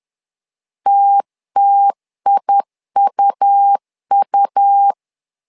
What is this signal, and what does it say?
Morse code, heard as a keyed tone, about 10 wpm, 800 Hz. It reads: TTIUU
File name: LEZIONI / 2 UHDN / 2 U 10wpm 1622